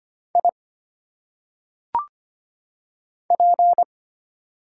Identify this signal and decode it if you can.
Morse code, heard as a keyed tone, about 25 wpm, 700 Hz. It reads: I P
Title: Single Letters - Phonetic Pronunciation 25wpm